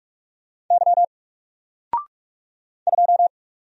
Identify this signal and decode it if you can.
Morse code, heard as a keyed tone, about 45 wpm, 700 Hz. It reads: Y 2
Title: Single Letter-Number 45wpm